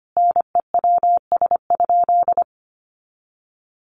Morse code, heard as a keyed tone, about 25 wpm, 700 Hz. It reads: NEWS?
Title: Sentences from Top 800 Words 25wpm